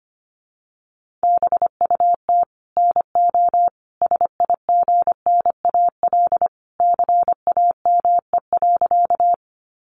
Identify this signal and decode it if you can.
Morse code, heard as a keyed tone, about 25 wpm, 700 Hz. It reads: BUT NO SIGNAL CAME.
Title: Sentences from Top 1400 Words 25wpm